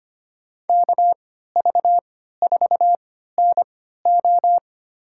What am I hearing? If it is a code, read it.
Morse code, heard as a keyed tone, about 25 wpm, spaced as for 14 wpm, 700 Hz. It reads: KV4NO
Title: USA Call Signs - 5 Characters - 3x Character Spacing - 25wpm